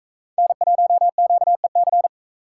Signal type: Morse code

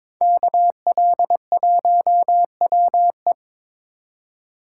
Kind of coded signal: Morse code

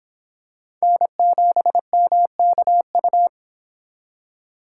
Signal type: Morse code

